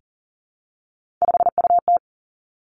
Morse code, heard as a keyed tone, about 38 wpm, 700 Hz. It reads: 5UT
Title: Three Challenging Letter-Number Combinations 40wpm